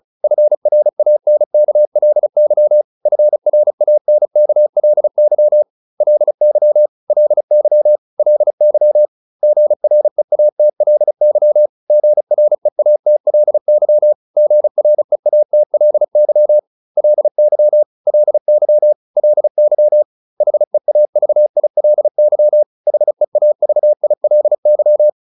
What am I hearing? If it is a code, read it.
Morse code, heard as a keyed tone, about 34 wpm, 600 Hz. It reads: FRANKLY FRANKLY LY LY LY GREATLY GREATLY GREATLY LY LY LY HEAVILY HEAVILY
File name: LY7 35